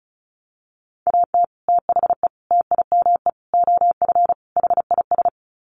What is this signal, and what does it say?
Morse code, heard as a keyed tone, about 34 wpm, 700 Hz. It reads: AT THE TIME OF HIS
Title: Sets of 5 Words Encore2 35wpm